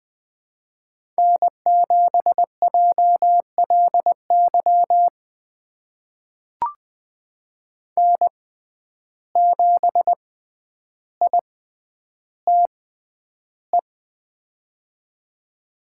Morse code, heard as a keyed tone, about 20 wpm, spaced as for 10 wpm, 700 Hz. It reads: N7JLY N7ITE
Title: USA Call Signs - 5 Characters - 6x Character Spacing - 20wpm